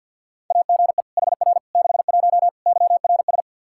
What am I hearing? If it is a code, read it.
Morse code, heard as a keyed tone, about 52 wpm, 700 Hz. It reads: AGE HR 61 YRS
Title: Practice QSO - Exchange 3 - More Info - 50wpm